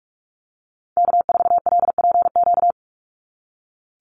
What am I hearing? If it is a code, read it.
Morse code, heard as a keyed tone, about 44 wpm, 700 Hz. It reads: K4LPQ
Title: NA SOTA Activator Callsigns 45wpm